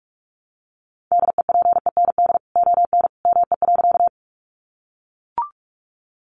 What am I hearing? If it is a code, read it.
Morse code, heard as a keyed tone, about 44 wpm, 700 Hz. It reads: DEPEND ON ME.
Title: Sentences from Top 1300 Words 45wpm